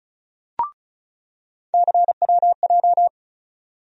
Morse code, heard as a keyed tone, about 34 wpm, 700 Hz. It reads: CWJ
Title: USA Call Sign Postfixes - 3 Letters 35wpm